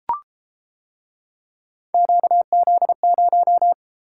Morse code, heard as a keyed tone, about 30 wpm, 700 Hz. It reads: QZ0